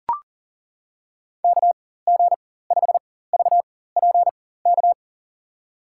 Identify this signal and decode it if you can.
Morse code, heard as a keyed tone, about 37 wpm, spaced as for 18 wpm, 700 Hz. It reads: KG5VPK
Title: USA Call Signs - 4x Character Spacing - 40wpm